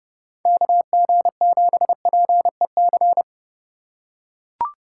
Morse code, heard as a keyed tone, about 28 wpm, 700 Hz. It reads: KG7PEC